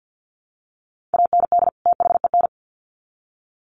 Morse code, heard as a keyed tone, about 50 wpm, 700 Hz. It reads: AND THEN
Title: Sets of 2 Words 50wpm